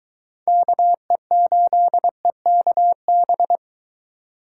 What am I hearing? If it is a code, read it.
Morse code, heard as a keyed tone, about 24 wpm, 700 Hz. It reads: KE8EKB